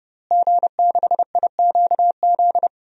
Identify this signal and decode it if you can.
Morse code, heard as a keyed tone, about 30 wpm, 700 Hz. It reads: G6IQZ
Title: DXCC-100 Call Signs - 5 Characters - Repeated 2x - 30wpm